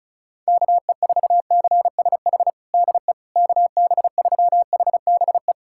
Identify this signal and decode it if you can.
Morse code, heard as a keyed tone, about 36 wpm, 700 Hz. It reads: KE4CSH DE KB3HBE
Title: USA Callsign DE Callsign 35wpm